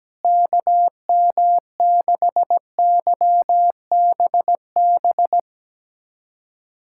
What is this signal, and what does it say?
Morse code, heard as a keyed tone, about 17 wpm, 700 Hz. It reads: KM6YBB
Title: USA Callsign DE Callsign - 6 Characters - 17wpm